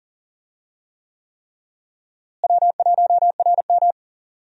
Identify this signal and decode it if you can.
Morse code, heard as a keyed tone, about 40 wpm, 700 Hz. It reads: W1RM
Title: USA Call Signs - 4 Characters - Repeated 3x - 40wpm